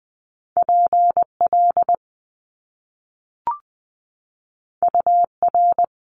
Morse code, heard as a keyed tone, about 20 wpm, 700 Hz. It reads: PL UR
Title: Top Two Letter Combinations in Words 20wpm